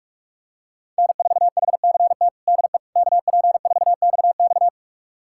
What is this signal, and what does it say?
Morse code, heard as a keyed tone, about 46 wpm, 700 Hz. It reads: N4HCT DE KP4XX
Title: USA Callsign DE Callsign - 5 Characters - 45wpm